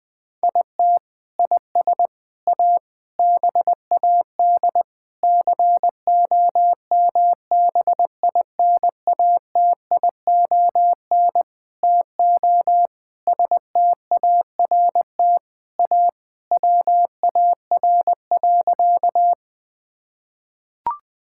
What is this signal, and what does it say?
Morse code, heard as a keyed tone, about 20 wpm, 700 Hz. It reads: IT IS A BAD COMBINATION TO START A WAR.
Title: Sentences from Top 1000 Words - Review Part 6 - 20wpm